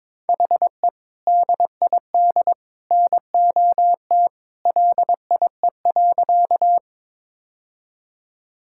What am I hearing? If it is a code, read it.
Morse code, heard as a keyed tone, about 21 wpm, 700 Hz. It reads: HE DID NOT LIE.
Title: Sentences from Top 1600 Words 22wpm